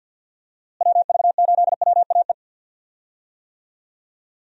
Morse code, heard as a keyed tone, about 49 wpm, 700 Hz. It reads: WV8WAE
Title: USA Call Signs - 6 Characters - Repeated 2x - 50wpm